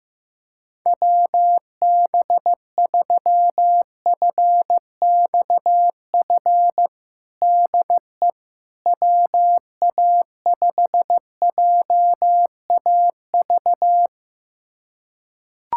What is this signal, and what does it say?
Morse code, heard as a keyed tone, about 15 wpm, 700 Hz. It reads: WB3FXF DE WA5JAV
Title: USA Callsign DE Callsign 15wpm